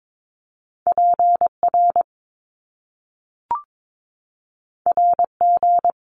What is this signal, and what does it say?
Morse code, heard as a keyed tone, about 22 wpm, 700 Hz. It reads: PR RG
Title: Top Two Letter Combinations in Words 22wpm